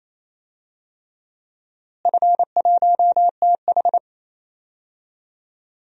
Morse code, heard as a keyed tone, about 28 wpm, 700 Hz. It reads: F1TH
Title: DXCC-100 Call Signs - 4 Characters - Repeated 3x - 28wpm